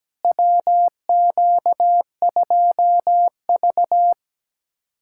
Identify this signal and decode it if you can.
Morse code, heard as a keyed tone, about 17 wpm, 700 Hz. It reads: WQ2V
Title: USA Callsign DE Callsign - 4 Characters - 17wpm